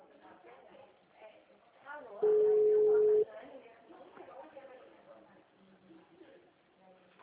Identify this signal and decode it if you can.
Ringback tone (Europe, 425 Hz), with speech.
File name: out- -2014- - - .27719